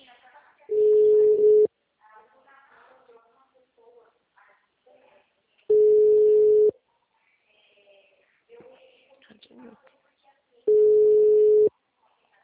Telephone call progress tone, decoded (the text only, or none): ringback tone